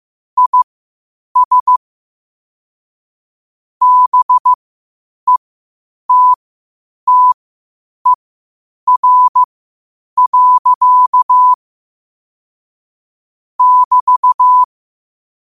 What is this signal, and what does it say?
Morse code, heard as a keyed tone, about 15 wpm, spaced as for 8 wpm, 1000 Hz. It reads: IS BETTER. <BT>
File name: qotdfiles / 8wpm